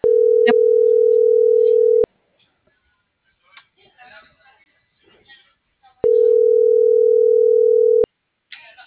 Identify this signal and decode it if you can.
Ringback tone (North America), 440 + 480 Hz.
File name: CENTRAL FRALSE / out brendaaviles.j